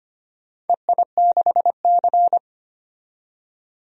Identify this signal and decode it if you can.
Morse code, heard as a keyed tone, about 25 wpm, 700 Hz. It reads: EI6C
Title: DXCC-100 Call Signs - 4 Characters - 25wpm